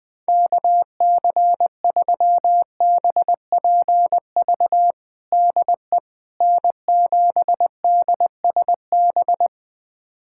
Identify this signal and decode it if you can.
Morse code, heard as a keyed tone, about 20 wpm, 700 Hz. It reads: KC3BPV DE N7DSB
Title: USA Callsign DE Callsign 20wpm